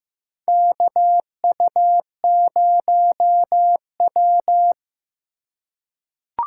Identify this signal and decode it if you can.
Morse code, heard as a keyed tone, about 15 wpm, 700 Hz. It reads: KU0W
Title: USA Call Signs - 4 Characters - 2x Character Spacing - 15wpm